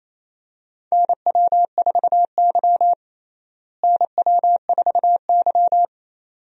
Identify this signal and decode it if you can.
Morse code, heard as a keyed tone, about 28 wpm, 700 Hz. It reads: NW4Y NW4Y
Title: USA Call Signs - 4 Characters - Repeated 2x - 28wpm